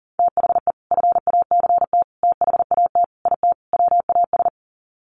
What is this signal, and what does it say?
Morse code, heard as a keyed tone, about 40 wpm, 700 Hz. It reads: THE FACT THAT IT WAS
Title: Sets of 5 Words Encore2 40wpm